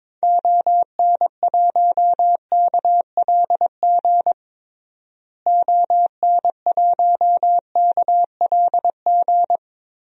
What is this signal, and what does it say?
Morse code, heard as a keyed tone, about 22 wpm, 700 Hz. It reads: ON1KLG ON1KLG
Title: DXCC-100 Call Signs - 6 Characters - Repeated 3x - 22wpm